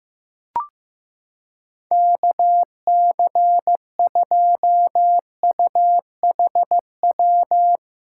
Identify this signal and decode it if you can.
Morse code, heard as a keyed tone, about 15 wpm, 700 Hz. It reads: KC2UHW
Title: Practice QSO - Exchange 4 - Ending - 15wpm